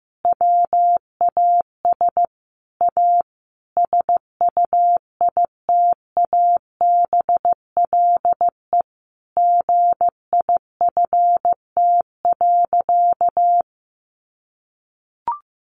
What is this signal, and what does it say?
Morse code, heard as a keyed tone, about 15 wpm, 700 Hz. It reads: WAS A SUITABLE GIFT.
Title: Sentences from Top 1400 Words 15wpm